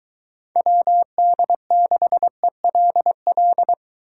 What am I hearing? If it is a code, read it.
Morse code, heard as a keyed tone, about 24 wpm, 700 Hz. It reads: WD6ELL